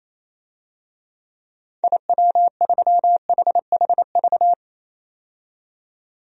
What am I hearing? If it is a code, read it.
Morse code, heard as a keyed tone, about 28 wpm, 700 Hz. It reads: IW3HHV
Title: DXCC-100 Call Signs - 6 Characters - 28wpm